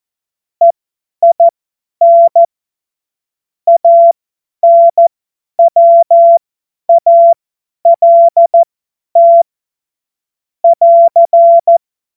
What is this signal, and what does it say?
Morse code, heard as a keyed tone, about 14 wpm, spaced as for 10 wpm, 680 Hz. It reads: EIN ANWALT <AR>
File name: Morsekurs / Witze / Witz 25. Oktober 2022 50 F70